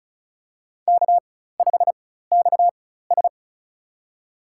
Morse code, heard as a keyed tone, about 34 wpm, spaced as for 16 wpm, 690 Hz. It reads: K5XS
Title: USA Call Signs - 4x Character Spacing - 35wpm